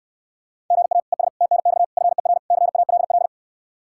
Morse code, heard as a keyed tone, about 69 wpm, 700 Hz. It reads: GA ES TNX FER CALL